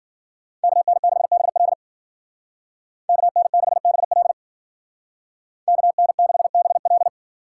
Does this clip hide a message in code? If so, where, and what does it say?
Yes: Morse code from 0.6 to 7.1 s, heard as a keyed tone, about 50 wpm, 700 Hz. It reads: KN6BL KN6BL KN6BL